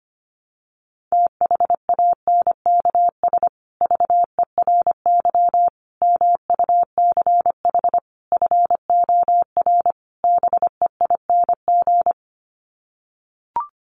Morse code, heard as a keyed tone, about 25 wpm, 700 Hz. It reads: THANKS VERY MUCH FOR BEING
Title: Sets of 5 Words Encore 25wpm